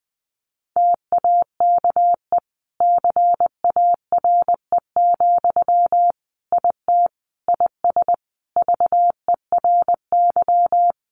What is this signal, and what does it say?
Morse code, heard as a keyed tone, about 20 wpm, 700 Hz. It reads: TAKE CARE, IT IS VERY
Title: Sentences from Top 1100 Words 20wpm